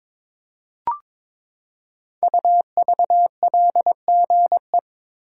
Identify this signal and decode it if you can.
Morse code, heard as a keyed tone, about 22 wpm, 700 Hz. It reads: UVLGE
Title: Code Groups - 5 Letter:Numbers - 22wpm